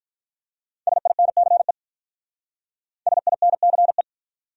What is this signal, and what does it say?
Morse code, heard as a keyed tone, about 46 wpm, 700 Hz. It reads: SINCE SINCE